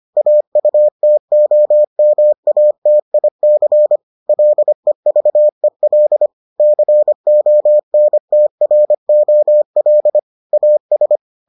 Morse code, heard as a keyed tone, about 25 wpm, 600 Hz. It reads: AUTOMATIC LEVEL CONTROL AS